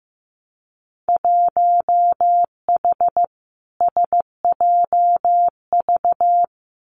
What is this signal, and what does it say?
Morse code, heard as a keyed tone, about 15 wpm, 700 Hz. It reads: 1H SJV
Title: Winter Field Day Exchanges 15wpm